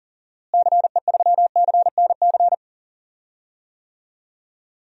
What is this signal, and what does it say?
Morse code, heard as a keyed tone, about 40 wpm, 700 Hz. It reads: CE3CNC
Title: DXCC-100 Call Signs - 6 Characters - Repeated 3x - 40wpm